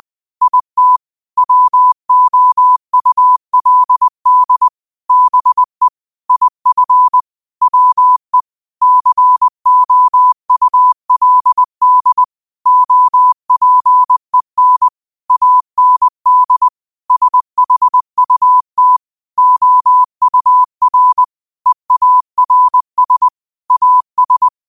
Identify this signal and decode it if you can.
Morse code, heard as a keyed tone, about 20 wpm, 1000 Hz. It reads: IT WOULD BE IF WE COULD OPEN AND SHUT OUR EARS AS
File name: qotdfiles / 20wpm